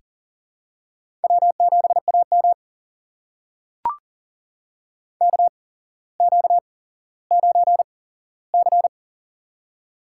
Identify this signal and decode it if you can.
Morse code, heard as a keyed tone, about 40 wpm, spaced as for 17 wpm, 700 Hz. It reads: W7AM KQ9C
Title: USA Call Signs - 4 Characters - 8x Character Spacing - 40wpm